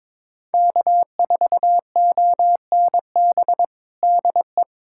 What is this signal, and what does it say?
Morse code, heard as a keyed tone, about 21 wpm, 700 Hz. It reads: K4ONB DE
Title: USA Callsign DE Callsign 22wpm